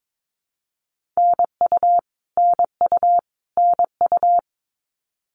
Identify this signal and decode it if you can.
Morse code, heard as a keyed tone, about 22 wpm, 700 Hz. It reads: NU NU NU
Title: Canadian Provinces and Territories - Repeated 3x - 22wpm